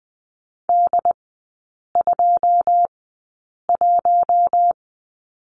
Morse code, heard as a keyed tone, about 20 wpm, spaced as for 8 wpm, 700 Hz. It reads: D21
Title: 114 Lesson 19e - Letter w - Three Characters 20wpm